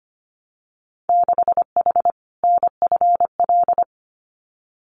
Morse code, heard as a keyed tone, about 25 wpm, 700 Hz. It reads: 6H NFL
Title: Winter Field Day Exchanges Encore 25wpm